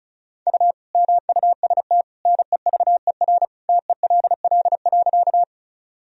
Morse code, heard as a keyed tone, about 36 wpm, 700 Hz. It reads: U MUST NEVER TELL.
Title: Sentences from Top 300 Words and Abbreviations 35wpm